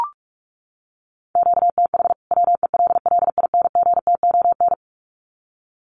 Morse code, heard as a keyed tone, about 44 wpm, 700 Hz. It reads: QTH WELLINGTON
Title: Common QSO Phrases 45wpm